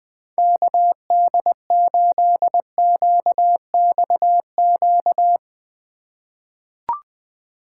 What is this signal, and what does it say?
Morse code, heard as a keyed tone, about 20 wpm, 700 Hz. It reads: KD8QXQ
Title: USA Call Signs - 6 Characters - 3x Character Spacing - 20wpm